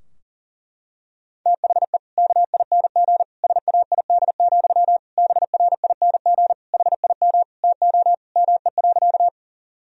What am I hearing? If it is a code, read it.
Morse code, heard as a keyed tone, about 40 wpm, 700 Hz. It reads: THE KING SAID, BRING HIM TO ME.
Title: Sentences from Top 1000 Words - Review Part 3 - 40wpm